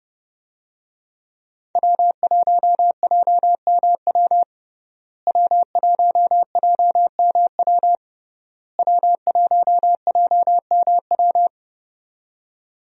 Morse code, heard as a keyed tone, about 30 wpm, 700 Hz. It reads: W1JMW W1JMW W1JMW
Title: USA Call Signs - 5 Characters - Repeated 3x - 30wpm